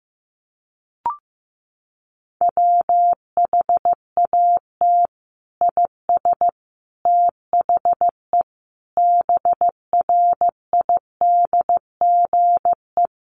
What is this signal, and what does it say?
Morse code, heard as a keyed tone, about 14 wpm, 700 Hz. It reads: WHAT IS THE BRIDGE
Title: Sentences from Top 1200 Words 15wpm